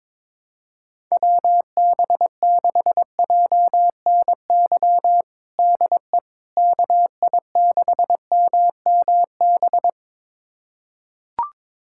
Morse code, heard as a keyed tone, about 22 wpm, 700 Hz. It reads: WB6JNY DE KI6MMB
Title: USA Callsign DE Callsign 22wpm